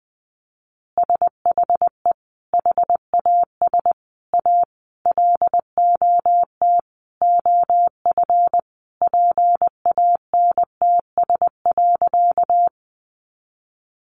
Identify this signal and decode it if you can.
Morse code, heard as a keyed tone, about 19 wpm, 700 Hz. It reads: SHE HAS A LOT OF PANTS.
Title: Additional Easy Phrases 20wpm